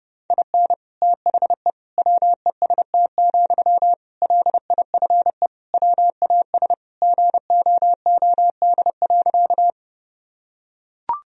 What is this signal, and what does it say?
Morse code, heard as a keyed tone, about 30 wpm, 700 Hz. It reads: IN THE WEST, LIFE WAS GOOD.
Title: Sentences from Top 1000 Words - Review Part 2 - 30wpm